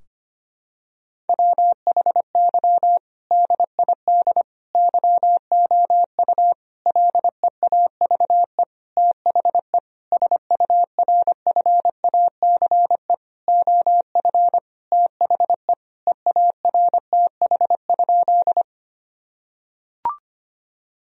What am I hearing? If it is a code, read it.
Morse code, heard as a keyed tone, about 25 wpm, 700 Hz. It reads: WHY DID YOU LEAVE THE SURFACE OF THE EARTH?
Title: Sentences from Top 1000 Words - Review Part 3 - 25wpm